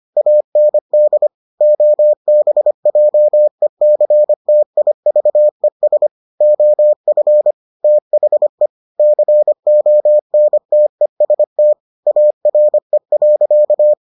Morse code, heard as a keyed tone, about 25 wpm, 600 Hz. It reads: AND OBJECTIVES OF THE CONTEST ARE.